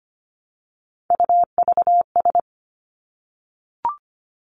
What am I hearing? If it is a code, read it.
Morse code, heard as a keyed tone, about 25 wpm, 700 Hz. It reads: UVS
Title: Three Challenging Letter-Number Combinations 25wpm